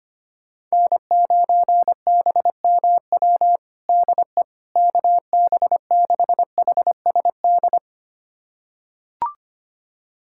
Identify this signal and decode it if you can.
Morse code, heard as a keyed tone, about 25 wpm, 700 Hz. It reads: N9BMW DE KB6HSD
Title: USA Callsign DE Callsign 25wpm